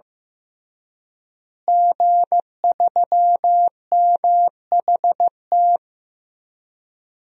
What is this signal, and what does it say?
Morse code, heard as a keyed tone, about 15 wpm, 700 Hz. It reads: G3MHT
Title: DXCC-100 Call Signs - 5 Characters - Repeated 3x - 15wpm